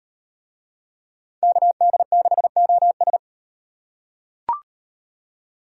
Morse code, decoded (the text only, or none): KD6OS